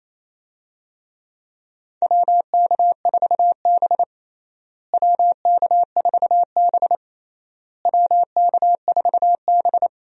Morse code, heard as a keyed tone, about 28 wpm, 700 Hz. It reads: WK4B WK4B WK4B
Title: USA Call Signs - 4 Characters - Repeated 3x - 28wpm